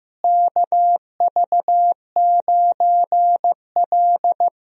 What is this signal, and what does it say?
Morse code, heard as a keyed tone, about 15 wpm, 700 Hz. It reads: KV9L
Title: USA Call Signs - 4 Characters - Repeated 2x - 15wpm